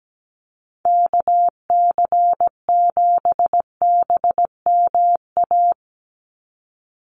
Morse code, heard as a keyed tone, about 17 wpm, 700 Hz. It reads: KC7BMA
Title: USA Call Signs 17wpm